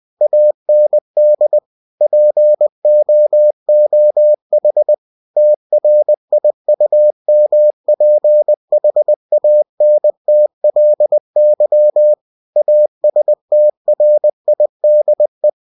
Morse code, heard as a keyed tone, about 20 wpm, 600 Hz. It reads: AND POOH TRIUMPHANTLY ASTRIDE
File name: cw-books / winnie-the-pooh / winnie-20wpm-page-0095